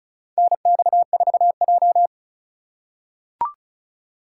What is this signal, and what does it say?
Morse code, heard as a keyed tone, about 36 wpm, 700 Hz. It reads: NX4J
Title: USA Call Signs - 4 Characters - 4x Character Spacing - 35wpm